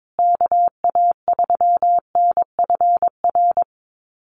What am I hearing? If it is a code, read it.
Morse code, heard as a keyed tone, about 22 wpm, 700 Hz. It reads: KA3NFR
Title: USA Call Signs Repeated 3x 22wpm